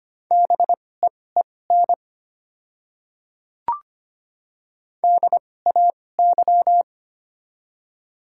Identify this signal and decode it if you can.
Morse code, heard as a keyed tone, about 25 wpm, spaced as for 18 wpm, 690 Hz. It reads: BEEN DAY
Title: 2x Character Spacing - Top 100 Words - 25wpm